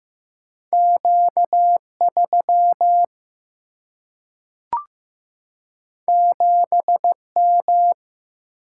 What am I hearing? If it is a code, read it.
Morse code, heard as a keyed tone, about 15 wpm, 700 Hz. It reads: Q3 7M
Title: Single-Letter Single-Number 15wpm